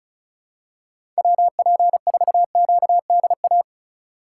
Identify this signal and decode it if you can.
Morse code, heard as a keyed tone, about 36 wpm, 700 Hz. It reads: WP4QDA
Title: USA Call Signs - 6 Characters - 3x Character Spacing - 35wpm